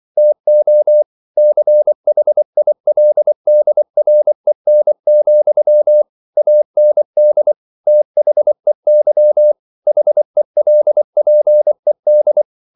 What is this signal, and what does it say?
Morse code, heard as a keyed tone, about 24 wpm, 600 Hz. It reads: TO CHILDREN, AND THEY HELPED